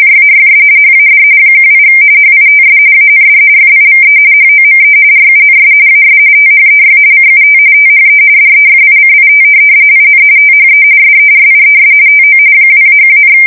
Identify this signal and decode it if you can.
RTTY, about 45.45 baud. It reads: JAYS HAD 18 HITS AND FIVE HOME RUNS AS THEY CUT THE DEFICIT IN THE SERIES TO 2-1,
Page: 170 Shift 2125/2295